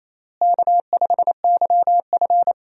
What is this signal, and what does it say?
Morse code, heard as a keyed tone, about 28 wpm, 700 Hz. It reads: K5YF
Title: USA Call Signs - 4 Characters - Repeated 2x - 28wpm